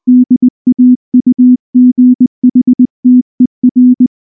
Morse code, heard as a keyed tone, about 19 wpm, 250 Hz. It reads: DAUGHTER
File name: morse / morse-training-20WPM-7